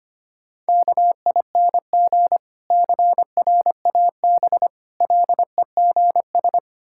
Morse code, heard as a keyed tone, about 25 wpm, 700 Hz. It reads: KING CRAB LEGS
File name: practice / 25wpm / Difficult Words with Hints - 25wpm